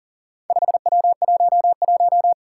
Morse code, heard as a keyed tone, about 40 wpm, 700 Hz. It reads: 5W11
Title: Four Challenging Letter-Number Combinations 40wpm